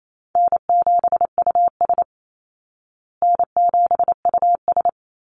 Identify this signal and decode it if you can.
Morse code, heard as a keyed tone, about 28 wpm, 700 Hz. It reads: N7US N7US
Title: Call Signs Repeated 3x - 28wpm